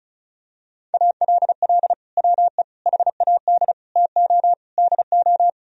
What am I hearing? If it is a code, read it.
Morse code, heard as a keyed tone, about 36 wpm, 700 Hz. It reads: ALL WE HAD TO DO
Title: Sentences from Top 1000 Words - Review Part 4 - 35wpm